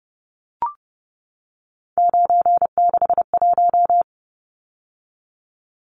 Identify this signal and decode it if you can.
Morse code, heard as a keyed tone, about 30 wpm, 700 Hz. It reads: 961
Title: Numbers - 3-Digit 30wpm